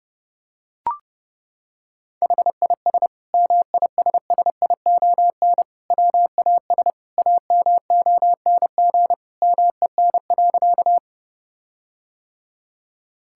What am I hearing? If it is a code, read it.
Morse code, heard as a keyed tone, about 30 wpm, 700 Hz. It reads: HIS MISSION WAS AMONG MEN.
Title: Sentences from Top 1600 Words 30wpm